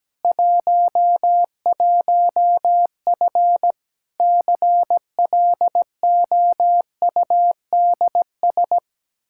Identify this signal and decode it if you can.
Morse code, heard as a keyed tone, about 17 wpm, 700 Hz. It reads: 11F CLOUDS
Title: Practice QSO - Exchange 2 - Info - 17wpm